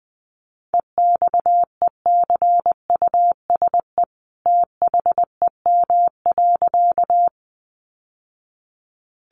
Morse code, heard as a keyed tone, about 20 wpm, 700 Hz. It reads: EXECUSE THEM.
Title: Sentences from Top 700 Words 20wpm